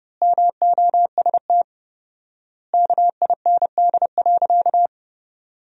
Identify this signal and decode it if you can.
Morse code, heard as a keyed tone, about 30 wpm, 700 Hz. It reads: MOST KIND.
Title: 4x Word-Spacing - Sentences from Top 200 Words - 30wpm